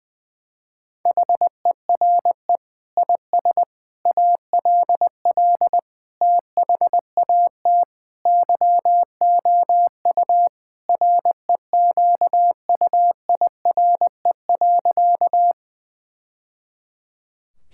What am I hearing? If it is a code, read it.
Morse code, heard as a keyed tone, about 20 wpm, 700 Hz. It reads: HERE IS ALL THAT YOU REQUIRE.
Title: Sentences from Top 1000 Words - Review Part 6 - 20wpm